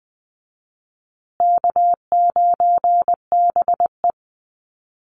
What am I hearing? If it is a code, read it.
Morse code, heard as a keyed tone, about 20 wpm, 700 Hz. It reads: K9BE
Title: USA Call Signs Repeated 2x 20wpm